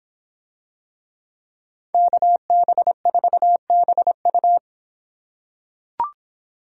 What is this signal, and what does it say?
Morse code, heard as a keyed tone, about 26 wpm, 700 Hz. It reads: KB4BU